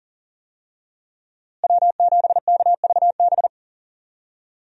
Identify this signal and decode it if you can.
Morse code, heard as a keyed tone, about 41 wpm, 700 Hz. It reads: W7KVB
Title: USA Call Signs - 5 Characters - Repeated 3x - 40wpm